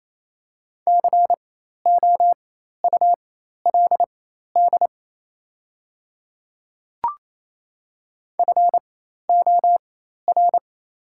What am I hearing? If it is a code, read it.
Morse code, heard as a keyed tone, about 28 wpm, spaced as for 13 wpm, 690 Hz. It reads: COULD FOR
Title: 4x Character Spacing - Top 200 Words - 28wpm